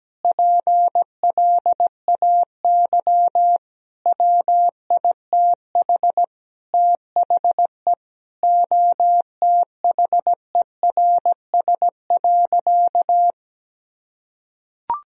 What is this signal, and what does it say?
Morse code, heard as a keyed tone, about 17 wpm, 700 Hz. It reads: PLAY WITH THE OTHERS.
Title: Sentences from Top 1000 Words - Review Part 2 - 17wpm